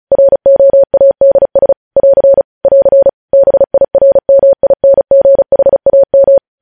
Morse code, heard as a keyed tone, about 36 wpm, 550 Hz. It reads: ROADS <AR> <AR> BIRMINGHAM
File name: podcast / News headlines-1x-35x35